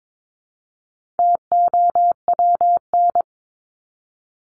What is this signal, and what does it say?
Morse code, heard as a keyed tone, about 22 wpm, 700 Hz. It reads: TOWN
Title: Top 500 Words Repeated 22wpm